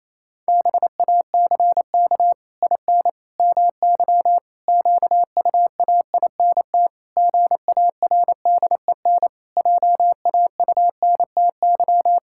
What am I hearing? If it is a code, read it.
Morse code, heard as a keyed tone, about 27 wpm, 700 Hz. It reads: BACK IN MY QUAINT GARDEN JAUNTY
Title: Pangram Challenge v2 - 28wpm